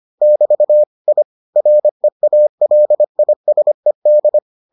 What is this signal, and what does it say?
Morse code, heard as a keyed tone, about 25 wpm, 600 Hz. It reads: <BT> I REALISED